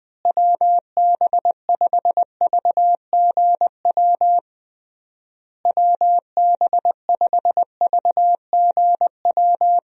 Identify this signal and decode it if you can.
Morse code, heard as a keyed tone, about 21 wpm, 700 Hz. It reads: WB5VGW WB5VGW
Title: USA Call Signs - 6 Characters - Repeated 3x - 20wpm